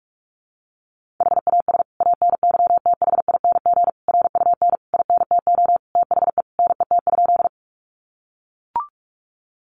Morse code, heard as a keyed tone, about 44 wpm, 700 Hz. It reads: HAS ANYTHING RUN INTO THE NET?
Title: Sentences from Top 1400 Words 45wpm